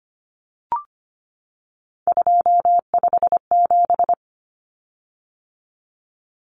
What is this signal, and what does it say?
Morse code, heard as a keyed tone, about 25 wpm, 700 Hz. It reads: 257
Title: Common Cut Numbers - 3-Digit 25wpm